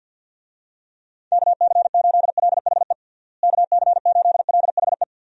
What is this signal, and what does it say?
Morse code, heard as a keyed tone, about 52 wpm, 700 Hz. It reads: KC8LHE KC8LHE
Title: USA Call Signs - 6 Characters - Repeated 2x - 50wpm